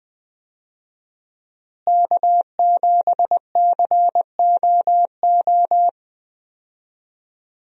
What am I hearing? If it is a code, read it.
Morse code, heard as a keyed tone, about 20 wpm, 700 Hz. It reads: K7COO
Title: USA Call Signs - 5 Characters - Repeated 3x - 20wpm